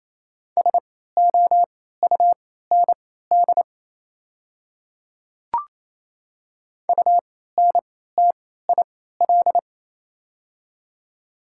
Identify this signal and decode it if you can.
Morse code, heard as a keyed tone, about 28 wpm, spaced as for 16 wpm, 700 Hz. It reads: SOUND UNTIL
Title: 3x Character Spacing - Top 300 Words - 28wpm